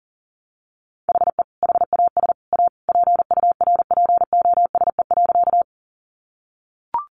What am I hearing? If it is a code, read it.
Morse code, heard as a keyed tone, about 40 wpm, 700 Hz. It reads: HE HAS A PURPOSE.
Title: Sentences from Top 800 Words 40wpm